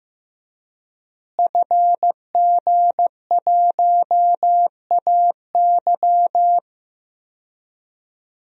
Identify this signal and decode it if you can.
Morse code, heard as a keyed tone, about 16 wpm, 700 Hz. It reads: FG1AY
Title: DXCC-100 Call Signs - 5 Characters - Repeated 3x - 15wpm